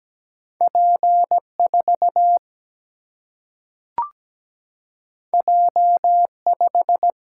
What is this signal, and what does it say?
Morse code, heard as a keyed tone, about 17 wpm, 700 Hz. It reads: P 4 J 5
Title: Single-Letter Single-Number 17wpm